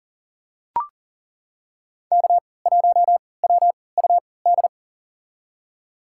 Morse code, heard as a keyed tone, about 40 wpm, spaced as for 23 wpm, 700 Hz. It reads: K1WUD
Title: USA Call Signs - 3x Character Spacing - 40wpm